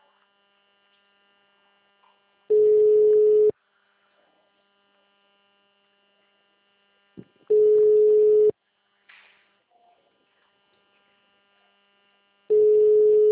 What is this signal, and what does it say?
Ringback tone (Europe, 425 Hz).